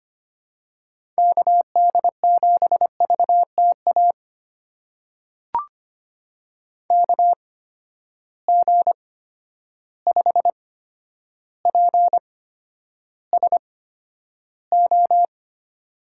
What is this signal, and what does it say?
Morse code, heard as a keyed tone, about 25 wpm, spaced as for 11 wpm, 700 Hz. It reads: KD7VTA KG5PSO
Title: USA Call Signs - 6 Characters - 8x Character Spacing - 25wpm